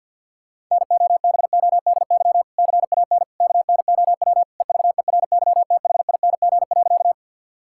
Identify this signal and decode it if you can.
Morse code, heard as a keyed tone, about 50 wpm, 700 Hz. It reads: NOBODY CAN KNOW EVERYTHING.
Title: Sentences from Top 1600 Words 50wpm